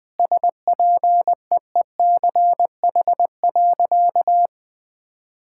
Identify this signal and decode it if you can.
Morse code, heard as a keyed tone, about 20 wpm, 700 Hz. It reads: SPEECH.
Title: Sentences with 5-Word Phrases - Spoken Beginning - 20wpm